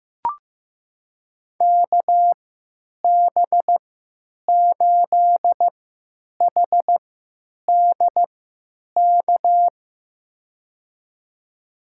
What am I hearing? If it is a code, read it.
Morse code, heard as a keyed tone, about 15 wpm, spaced as for 8 wpm, 700 Hz. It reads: KB8HDK
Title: USA Call Signs - 6 Characters - 3x Character Spacing - 15wpm